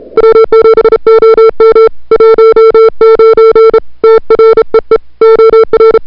Morse code, heard as a keyed tone, about 33 wpm, 440 Hz. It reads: W7OM 19 TREE OR
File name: CWSprint / February 2007 CW Sprint / 01-2007 CW Sprint Hour1